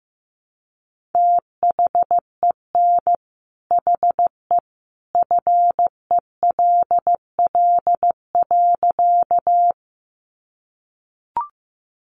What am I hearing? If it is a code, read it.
Morse code, heard as a keyed tone, about 15 wpm, 700 Hz. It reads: THEN HE FELL.
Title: A Sound Of Thunder 15wpm